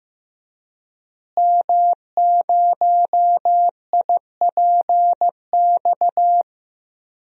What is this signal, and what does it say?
Morse code, heard as a keyed tone, about 16 wpm, 700 Hz. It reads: M0IPX
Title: DXCC-100 Call Signs - 5 Characters - Repeated 2x - 15wpm